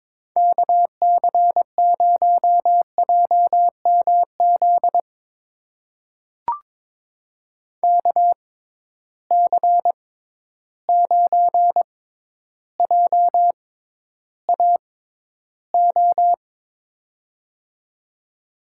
Morse code, heard as a keyed tone, about 22 wpm, spaced as for 11 wpm, 700 Hz. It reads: KC0JMZ KC9JAO
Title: USA Call Signs - 6 Characters - 6x Character Spacing - 22wpm